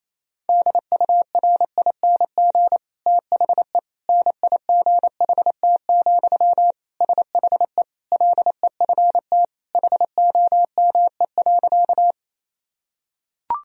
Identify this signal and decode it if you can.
Morse code, heard as a keyed tone, about 27 wpm, 700 Hz. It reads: DURING THE NIGHT, SHE LEFT HOME.
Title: Sentences from Top 1000 Words - Review Part 2 - 28wpm